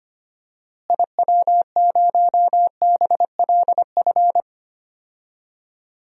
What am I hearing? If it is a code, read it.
Morse code, heard as a keyed tone, about 25 wpm, 700 Hz. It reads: IW0BLF
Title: DXCC-100 Call Signs - 6 Characters - Repeated 3x - 25wpm